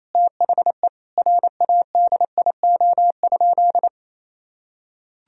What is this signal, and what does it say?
Morse code, heard as a keyed tone, about 28 wpm, 700 Hz. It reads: THE RADIO?
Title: Sentences from Top 1000 Words - Review Part 3 - 28wpm